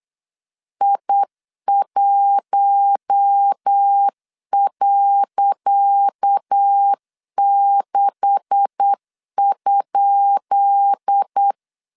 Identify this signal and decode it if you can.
Morse code, heard as a keyed tone, about 8 wpm, 800 Hz. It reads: I1.6?
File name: LEZIONI / 10 SIGNS / 10C DOT 8wpm 3649